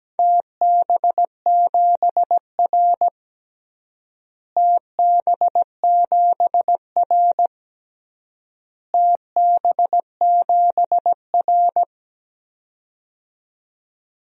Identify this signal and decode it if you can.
Morse code, heard as a keyed tone, about 17 wpm, 700 Hz. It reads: TB7R TB7R TB7R
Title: DXCC-100 Call Signs - 4 Characters - Repeated 3x - 17wpm